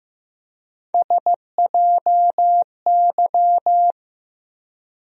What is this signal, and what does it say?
Morse code, heard as a keyed tone, about 15 wpm, 700 Hz. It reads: SJY
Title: USA Call Sign Postfixes - 3 Letters 15wpm